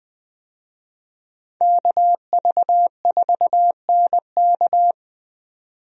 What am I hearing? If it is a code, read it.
Morse code, heard as a keyed tone, about 20 wpm, 700 Hz. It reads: KV4NK
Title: USA Call Signs - 5 Characters - 20wpm